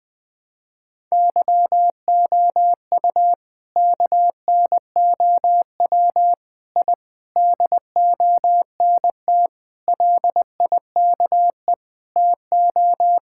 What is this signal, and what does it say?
Morse code, heard as a keyed tone, about 21 wpm, 700 Hz. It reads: YOU KNOW I DONT LIKE TO
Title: Sentences from Top 1000 Words - Review Part 6 - 20wpm